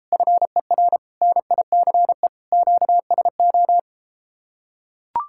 Morse code, heard as a keyed tone, about 33 wpm, 700 Hz. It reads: FER NICE QSO